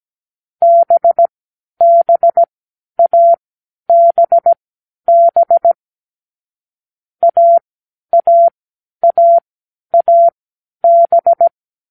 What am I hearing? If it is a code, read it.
Morse code, heard as a keyed tone, about 17 wpm, spaced as for 10 wpm, 680 Hz. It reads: BBABB AAAAB